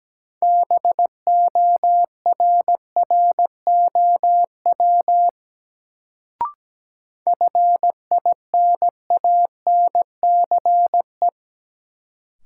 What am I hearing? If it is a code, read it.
Morse code, heard as a keyed tone, about 17 wpm, 700 Hz. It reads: BORROW FINANCE
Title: Sentences from Top 1500 Words - New Word Drill Alt - 17wpm